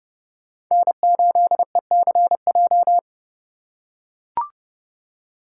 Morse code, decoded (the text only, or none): N8ECJ